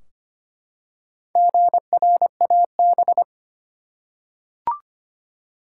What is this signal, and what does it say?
Morse code, heard as a keyed tone, about 25 wpm, 700 Hz. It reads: GRAB
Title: Four Letter Words - 25wpm